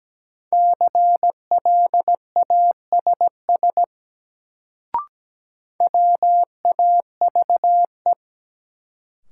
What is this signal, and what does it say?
Morse code, heard as a keyed tone, about 17 wpm, 700 Hz. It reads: CLASS WAVE
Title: Sentences from Top 1400 Words - New Word Drill Alt - 17wpm